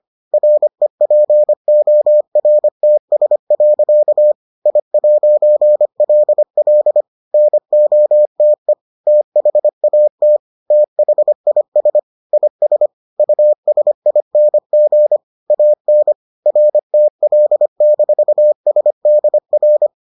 Morse code, heard as a keyed tone, about 25 wpm, 600 Hz. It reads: REPORTS. I'LL NOTE THAT THIS IS USING AN RTL-SDR